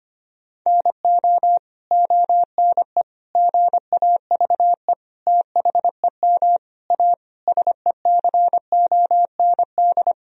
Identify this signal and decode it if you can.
Morse code, heard as a keyed tone, about 25 wpm, 700 Hz. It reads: NO ONE GAVE THEM A SECOND
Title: Sentences from Top 1000 Words - Review Part 1 - 25wpm